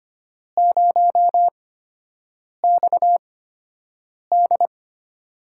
Morse code, heard as a keyed tone, about 25 wpm, spaced as for 7 wpm, 700 Hz. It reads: 0XD
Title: USA Call Signs - 8x Character Spacing - 25wpm